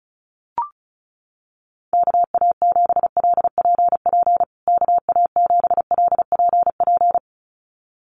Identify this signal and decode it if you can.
Morse code, heard as a keyed tone, about 34 wpm, 700 Hz. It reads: KA7LPP KA7LPP
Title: USA Call Signs Repeated 2x 35wpm